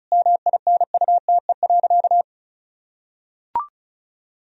Morse code, heard as a keyed tone, about 36 wpm, 700 Hz. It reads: MINUTE.
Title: Sentences from Top 500 Words and Abbreviations 35wpm